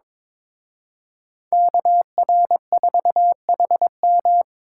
Morse code, heard as a keyed tone, about 25 wpm, 700 Hz. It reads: KR4HM
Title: Speed Racing - USA 2x2 Call Signs 22wpm